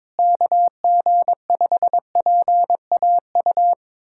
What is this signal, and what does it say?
Morse code, heard as a keyed tone, about 22 wpm, 700 Hz. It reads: KG5PAU
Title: USA Callsign DE Callsign 22wpm